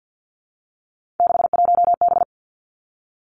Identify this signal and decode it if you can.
Morse code, heard as a keyed tone, about 50 wpm, 700 Hz. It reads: 61B
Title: Three Challenging Letter-Number Combinations 50wpm